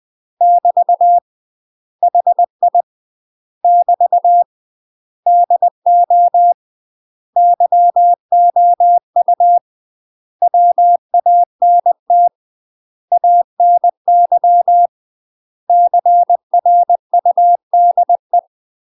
Morse code, heard as a keyed tone, about 20 wpm, 700 Hz. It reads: <BT> HI <BT> DO YOU WANT ANY CRUDE
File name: UK MASTER QSOs TRAINING / 20WPM / UK MASTER QSOs 0032